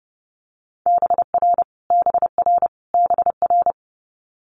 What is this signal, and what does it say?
Morse code, heard as a keyed tone, about 30 wpm, 700 Hz. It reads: BR BR BR
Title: Multi-letter Phonemes Repeated 3x 30wpm